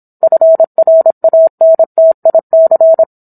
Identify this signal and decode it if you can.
Morse code, heard as a keyed tone, about 27 wpm, 650 Hz. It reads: FRANTIC